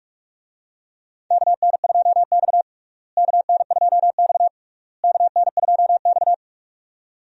Morse code, heard as a keyed tone, about 46 wpm, 700 Hz. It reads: KN2X KN2X KN2X
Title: USA Call Signs - 4 Characters - Repeated 3x - 45wpm